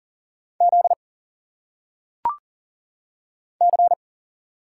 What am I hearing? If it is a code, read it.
Morse code, heard as a keyed tone, about 40 wpm, 700 Hz. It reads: Z C
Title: Single Letters - Phonetic Pronunciation 40wpm